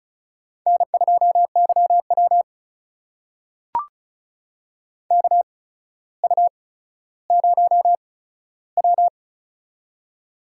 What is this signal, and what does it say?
Morse code, heard as a keyed tone, about 36 wpm, spaced as for 15 wpm, 700 Hz. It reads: N2YW KU0W
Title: USA Call Signs - 4 Characters - 8x Character Spacing - 35wpm